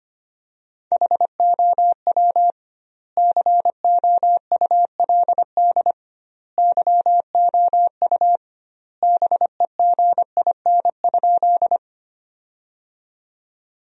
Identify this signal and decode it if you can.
Morse code, heard as a keyed tone, about 25 wpm, 700 Hz. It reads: HOW COULD YOU BEGIN?
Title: 2x Word-Spacing - Sentences from Top 300 Words - 25wpm